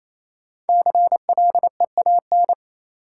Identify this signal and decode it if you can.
Morse code, heard as a keyed tone, about 28 wpm, 700 Hz. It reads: CLEAN
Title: Sentences from Top 1100 Words - New Word Drill - 28wpm